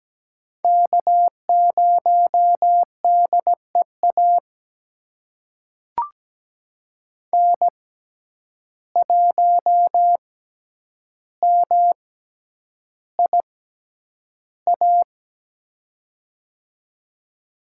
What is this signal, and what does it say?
Morse code, heard as a keyed tone, about 17 wpm, spaced as for 9 wpm, 700 Hz. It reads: K0DEA N1MIA
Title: USA Call Signs - 6x Character Spacing - 17wpm